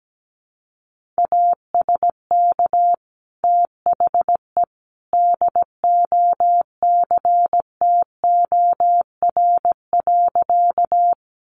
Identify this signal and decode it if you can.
Morse code, heard as a keyed tone, about 17 wpm, 700 Hz. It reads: ASK THE DOCTOR.
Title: Sentences from Top 1500 Words 17wpm